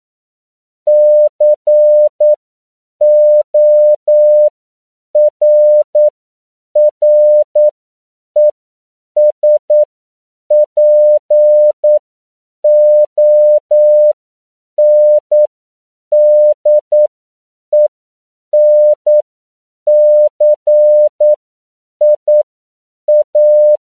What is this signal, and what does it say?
Morse code, heard as a keyed tone, about 9 wpm, spaced as for 7 wpm, 600 Hz. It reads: CORRESPONDENCIA
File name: CW / 10PPM